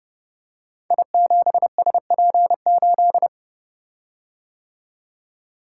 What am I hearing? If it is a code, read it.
Morse code, heard as a keyed tone, about 30 wpm, 700 Hz. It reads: I7SP8
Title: Code Groups - 5 Letter:Numbers - 30wpm